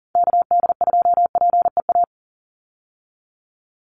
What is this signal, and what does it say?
Morse code, heard as a keyed tone, about 40 wpm, 700 Hz. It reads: KD2PEA
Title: USA Call Signs 40wpm